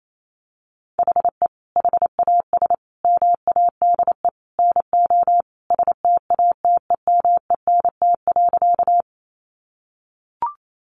Morse code, heard as a keyed tone, about 28 wpm, 700 Hz. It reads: HE HAS MADE NO STATEMENT.
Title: Sentences from Top 1200 Words 28wpm